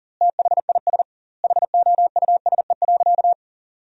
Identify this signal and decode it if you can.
Morse code, heard as a keyed tone, about 40 wpm, 700 Hz. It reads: THIS HOUSE.
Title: 2x Word-Spacing - Sentences from Top 400 Words - 40wpm